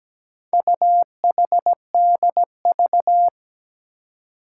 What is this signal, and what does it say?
Morse code, heard as a keyed tone, about 17 wpm, 700 Hz. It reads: UHDV
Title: Four Challenging Letter-Number Combinations 17wpm